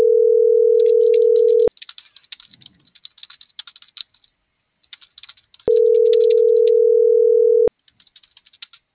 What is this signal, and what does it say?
Ringback tone (North America), 440 + 480 Hz.